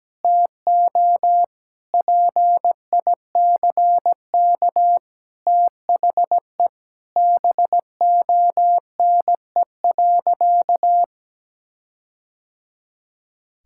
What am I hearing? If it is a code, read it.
Morse code, heard as a keyed tone, about 17 wpm, 700 Hz. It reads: TO PICK THE BONE.
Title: Sentences from Top 1600 Words 17wpm